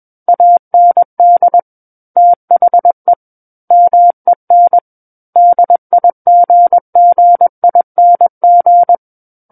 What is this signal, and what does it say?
Morse code, heard as a keyed tone, about 21 wpm, 700 Hz. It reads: AND THE MEN DIGGING
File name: Book One / 12-Chapter 12 - 21 WPM